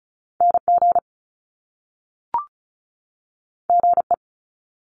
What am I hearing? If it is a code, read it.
Morse code, heard as a keyed tone, about 36 wpm, 700 Hz. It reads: NG GE
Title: Top Two Letter Combinations in Words 35wpm